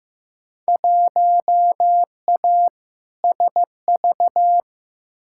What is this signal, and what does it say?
Morse code, heard as a keyed tone, about 15 wpm, 700 Hz. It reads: 1A SV
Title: Field Day Exchanges - 15wpm